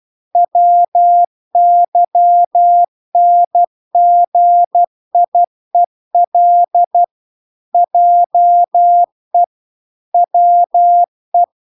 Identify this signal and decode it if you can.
Morse code, heard as a keyed tone, about 12 wpm, 700 Hz. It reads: WYNGIEL JE WE